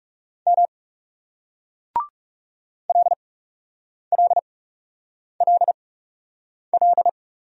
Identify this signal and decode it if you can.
Morse code, heard as a keyed tone, about 40 wpm, spaced as for 8 wpm, 700 Hz. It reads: M LLLL